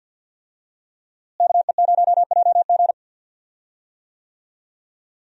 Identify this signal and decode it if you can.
Morse code, heard as a keyed tone, about 52 wpm, 700 Hz. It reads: KE0JG
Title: USA Call Signs - 5 Characters - Repeated 3x - 50wpm